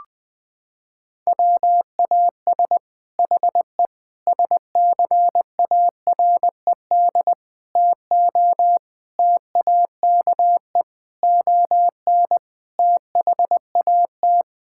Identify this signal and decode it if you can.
Morse code, heard as a keyed tone, about 21 wpm, 700 Hz. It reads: WAS HE SCARED TO TAKE ON THAT
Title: Sentences from Top 1000 Words - Review Part 6 - 20wpm